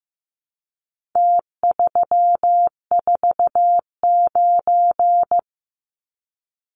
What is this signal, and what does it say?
Morse code, heard as a keyed tone, about 15 wpm, 700 Hz. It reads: T349
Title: Numbers - Contest Style - 4-Digit 15wpm